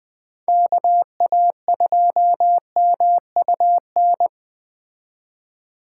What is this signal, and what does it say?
Morse code, heard as a keyed tone, about 20 wpm, 700 Hz. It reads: KA2MUN
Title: Speed Racing - USA 2x3 Call Signs 17wpm